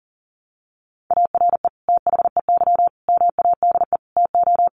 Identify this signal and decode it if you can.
Morse code, heard as a keyed tone, about 40 wpm, 700 Hz. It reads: ARE THEY MADE TO
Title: Sentences from Top 400 Words 40wpm